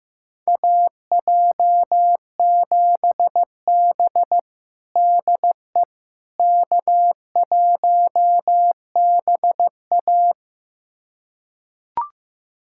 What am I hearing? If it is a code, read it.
Morse code, heard as a keyed tone, about 15 wpm, 700 Hz. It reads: AJ7B DE K1BA
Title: USA Callsign DE Callsign - 4 Characters - 15wpm